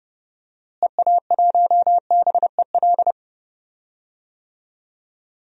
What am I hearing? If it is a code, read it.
Morse code, heard as a keyed tone, about 32 wpm, 700 Hz. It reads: EA1BEL
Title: DXCC-100 Call Signs - 6 Characters - Repeated 2x - 30wpm